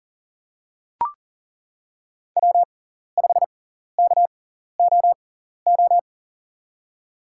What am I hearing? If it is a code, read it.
Morse code, heard as a keyed tone, about 40 wpm, spaced as for 14 wpm, 700 Hz. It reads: W5KOO
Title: USA Call Signs - 6x Character Spacing - 40wpm